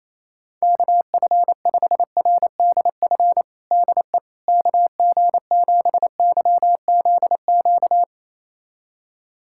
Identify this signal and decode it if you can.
Morse code, heard as a keyed tone, about 28 wpm, 700 Hz. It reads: KF5RDF DE KG7YZQ
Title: USA Callsign DE Callsign - 6 Characters - 28wpm